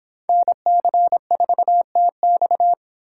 Morse code, heard as a keyed tone, about 24 wpm, 700 Hz. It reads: NC4TX